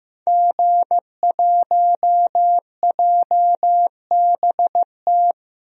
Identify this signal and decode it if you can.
Morse code, heard as a keyed tone, about 15 wpm, 700 Hz. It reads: G1JBT
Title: DXCC-100 Call Signs - 5 Characters - 15wpm